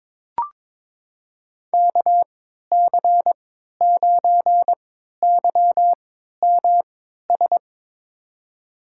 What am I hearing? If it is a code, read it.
Morse code, heard as a keyed tone, about 22 wpm, spaced as for 12 wpm, 700 Hz. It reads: KC9YMS
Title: USA Call Signs - 6 Characters - 3x Character Spacing - 22wpm